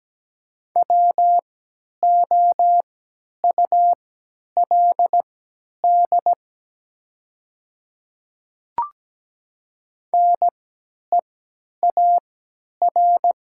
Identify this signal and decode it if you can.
Morse code, heard as a keyed tone, about 17 wpm, spaced as for 10 wpm, 700 Hz. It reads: WOULD NEAR
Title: 3x Character Spacing - Top 500 Words - 17wpm